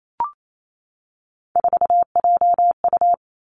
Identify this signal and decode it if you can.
Morse code, heard as a keyed tone, about 28 wpm, 700 Hz. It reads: 4JU
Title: Three Challenging Letter-Number Combinations 28wpm